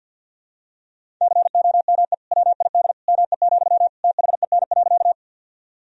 Morse code, heard as a keyed tone, about 50 wpm, 700 Hz. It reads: COME WID ME, THEN.
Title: Sentences from Top 200 Words and Abbreviations 50wpm